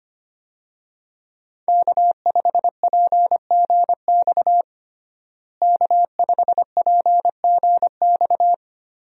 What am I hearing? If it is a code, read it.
Morse code, heard as a keyed tone, about 25 wpm, 700 Hz. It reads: K5PGX K5PGX
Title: USA Call Signs - 5 Characters - Repeated 2x - 25wpm